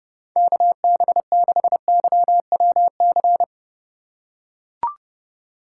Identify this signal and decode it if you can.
Morse code, heard as a keyed tone, about 29 wpm, 700 Hz. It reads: KB6YWC